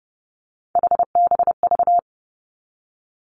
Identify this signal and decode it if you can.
Morse code, heard as a keyed tone, about 32 wpm, 700 Hz. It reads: HBV
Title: Three Challenging Letter-Number Combinations 30wpm